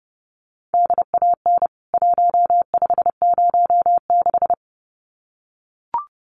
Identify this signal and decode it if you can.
Morse code, heard as a keyed tone, about 29 wpm, 700 Hz. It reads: DAN 1506
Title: CWT Exchange 30wpm